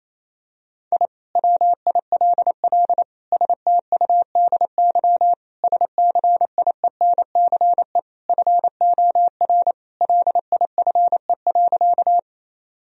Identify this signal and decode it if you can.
Morse code, heard as a keyed tone, about 28 wpm, 700 Hz. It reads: I WILL STUDY SCIENCE FOR LIFE.
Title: Sentences from Top 1000 Words - Review Part 2 - 28wpm